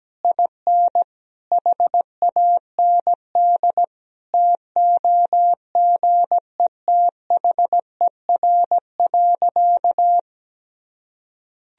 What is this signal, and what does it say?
Morse code, heard as a keyed tone, about 17 wpm, 700 Hz. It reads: IN HAND TOGETHER.
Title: Sentences from Top 1000 Words - Review Part 4 - 17wpm